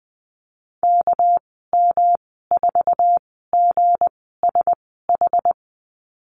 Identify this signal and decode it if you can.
Morse code, heard as a keyed tone, about 20 wpm, spaced as for 14 wpm, 700 Hz. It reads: KM4GSH
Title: 219 Lesson 35g - Number 0 - Call Signs 20wpm